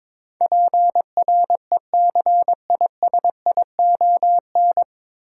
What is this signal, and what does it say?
Morse code, heard as a keyed tone, about 22 wpm, 700 Hz. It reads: PRECISION
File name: 9 Letter Words - First 3 Letters Given - 22wpm